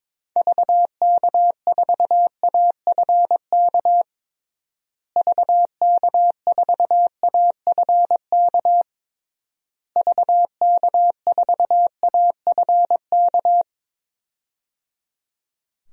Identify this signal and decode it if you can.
Morse code, heard as a keyed tone, about 22 wpm, 700 Hz. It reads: VK4AFK VK4AFK VK4AFK
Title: DXCC-100 Call Signs - 6 Characters - Repeated 3x - 22wpm